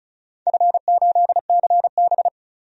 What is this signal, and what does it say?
Morse code, heard as a keyed tone, about 34 wpm, 700 Hz. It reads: F8CB
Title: DXCC-100 Call Signs - 4 Characters - 35wpm